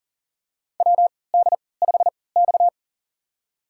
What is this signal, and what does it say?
Morse code, heard as a keyed tone, about 40 wpm, spaced as for 23 wpm, 700 Hz. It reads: WD5X
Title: USA Call Signs - 4 Characters - 3x Character Spacing - 40wpm